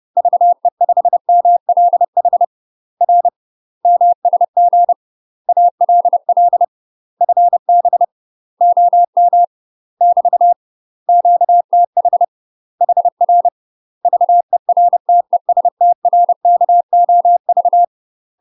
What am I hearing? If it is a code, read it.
Morse code, heard as a keyed tone, about 30 wpm, 700 Hz. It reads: VE5MLH R MSG ALL FB OM <BT> QTH HR VERTESTRKOV
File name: UK MASTER QSOs TRAINING / 30WPM / UK MASTER QSOs 0014